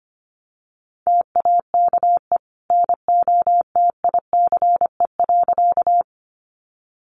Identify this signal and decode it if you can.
Morse code, heard as a keyed tone, about 25 wpm, 700 Hz. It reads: TAKE NOTICE.
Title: Sentences from Top 400 Words 25wpm